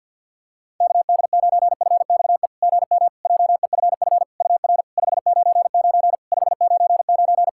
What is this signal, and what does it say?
Morse code, heard as a keyed tone, about 52 wpm, 690 Hz. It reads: KD9FXE GM JEFF UR 599 599
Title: SOTA QSO Practice 50wpm